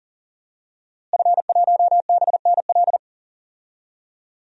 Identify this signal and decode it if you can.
Morse code, heard as a keyed tone, about 37 wpm, 700 Hz. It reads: F1BNL
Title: DXCC-100 Call Signs - 5 Characters - Repeated 2x - 40wpm